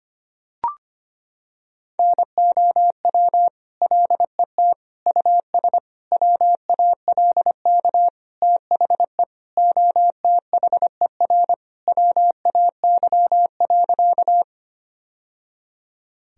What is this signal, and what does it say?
Morse code, heard as a keyed tone, about 25 wpm, 700 Hz. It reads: NOW LET US WALK THE OTHER WAY.
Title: Sentences from Top 1000 Words - Review Part 1 - 25wpm